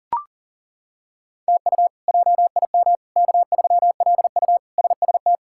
Morse code, heard as a keyed tone, about 37 wpm, 700 Hz. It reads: TU JIM K3LU SST
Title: K1USN SST QSO Practice - 40wpm